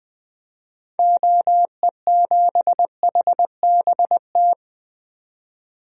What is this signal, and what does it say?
Morse code, heard as a keyed tone, about 20 wpm, 700 Hz. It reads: OE7HBT
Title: DXCC-100 Call Signs - 6 Characters - 20wpm